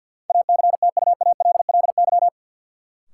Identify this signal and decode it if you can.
Morse code, heard as a keyed tone, about 53 wpm, 700 Hz. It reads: ACTUALLY
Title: Sentences from Top 700 Words - New Word Drill - 50wpm